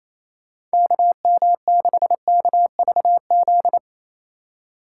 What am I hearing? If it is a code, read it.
Morse code, heard as a keyed tone, about 35 wpm, 700 Hz. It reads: KM6KVZ
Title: Speed Racing - USA 2x3 Call Signs 28wpm